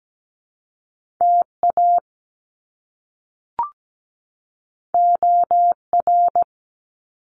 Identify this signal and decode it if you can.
Morse code, heard as a keyed tone, about 17 wpm, 700 Hz. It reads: TA OR
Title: Top Two Letter Combinations in Words 17wpm